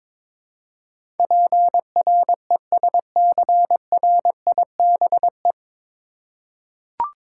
Morse code, heard as a keyed tone, about 22 wpm, 700 Hz. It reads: PRESCRIBE
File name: 9 Letter Words - First 3 Letters Given - 22wpm